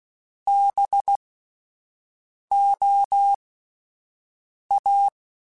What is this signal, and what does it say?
Morse code, heard as a keyed tone, about 16 wpm, 800 Hz. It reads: B O A